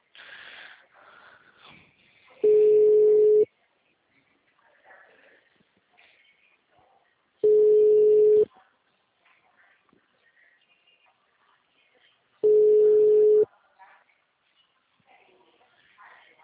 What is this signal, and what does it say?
Ringback tone (Europe, 425 Hz), with speech.
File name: out- -2012- - - .80007